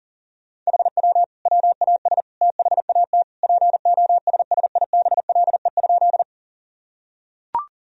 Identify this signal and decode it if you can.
Morse code, heard as a keyed tone, about 37 wpm, 700 Hz. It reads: HW WAS THAT POSSIBLE?
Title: Sentences from Top 500 Words and Abbreviations 40wpm